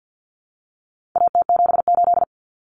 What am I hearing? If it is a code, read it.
Morse code, heard as a keyed tone, about 50 wpm, 700 Hz. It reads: AT78
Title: Common Cut Numbers - 4-Digit 50wpm